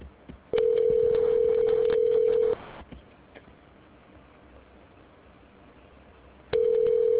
Ringback tone (North America), 440 + 480 Hz.